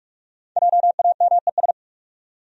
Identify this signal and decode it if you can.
Morse code, heard as a keyed tone, about 48 wpm, 700 Hz. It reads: JAMES
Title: USA Common Names - Max 5 Letters - 45wpm